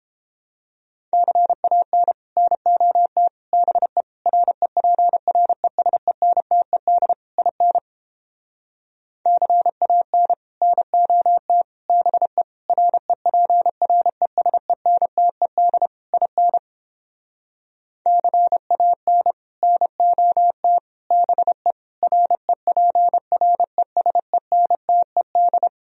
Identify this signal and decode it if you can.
Morse code, heard as a keyed tone, about 29 wpm, 700 Hz. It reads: CAN NOT BE REPRESENTED IN CAN NOT BE REPRESENTED IN CAN NOT BE REPRESENTED